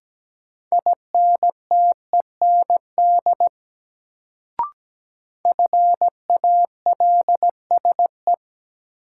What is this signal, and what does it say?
Morse code, heard as a keyed tone, about 17 wpm, 700 Hz. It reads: INTEND FALSE
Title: Sentences from Top 1600 Words - New Word Drill - 17wpm